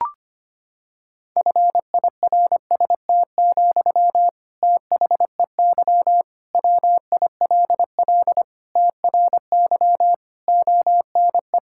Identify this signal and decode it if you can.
Morse code, heard as a keyed tone, about 25 wpm, 700 Hz. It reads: FIRST, THEY WILL TRY ONE
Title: Sentences from Top 1000 Words - Review Part 6 - 25wpm